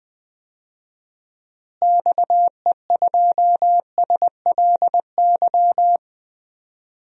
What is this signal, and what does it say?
Morse code, heard as a keyed tone, about 20 wpm, 700 Hz. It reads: XE2SLY
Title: DXCC-100 Call Signs - 6 Characters - Repeated 3x - 20wpm